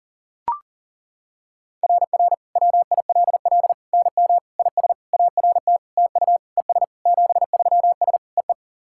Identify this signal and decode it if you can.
Morse code, heard as a keyed tone, about 40 wpm, 690 Hz. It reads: RR WILL NM IS ART TU ES 73S EE
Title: SOTA QSO Practice 40wpm